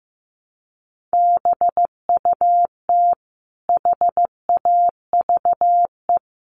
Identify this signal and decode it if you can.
Morse code, heard as a keyed tone, about 15 wpm, 700 Hz. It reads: BUT HAVE
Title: Sentences from Top 1500 Words 15wpm